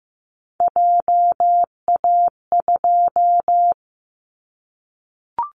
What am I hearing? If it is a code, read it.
Morse code, heard as a keyed tone, about 15 wpm, 700 Hz. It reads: JA2
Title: Callsign Prefixes for Top 50 DXCC - 15wpm